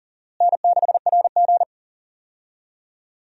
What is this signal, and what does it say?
Morse code, heard as a keyed tone, about 40 wpm, 700 Hz. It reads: N6RG
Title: USA Call Signs - 4 Characters - Repeated 2x - 40wpm